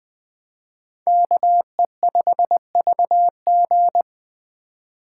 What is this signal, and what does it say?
Morse code, heard as a keyed tone, about 20 wpm, 700 Hz. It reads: KE5VG
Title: Speed Racing - USA 2x2 Call Signs 17wpm